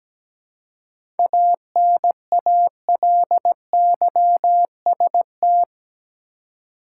Morse code, heard as a keyed tone, about 17 wpm, 700 Hz. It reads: ANALYST
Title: Seven Letter Words - 17wpm